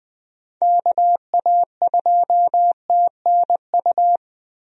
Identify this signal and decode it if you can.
Morse code, heard as a keyed tone, about 20 wpm, 700 Hz. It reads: KA2TNU
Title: USA Call Signs - 6 Characters - Repeated 2x - 20wpm